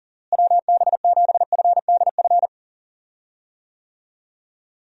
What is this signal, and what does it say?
Morse code, heard as a keyed tone, about 40 wpm, 700 Hz. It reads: WB7FDF
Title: USA Call Signs - 6 Characters - Repeated 2x - 40wpm